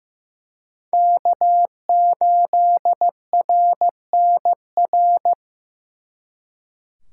Morse code, heard as a keyed tone, about 18 wpm, 700 Hz. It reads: K8RNR
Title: Speed Racing - USA 1x3 Call Signs 15wpm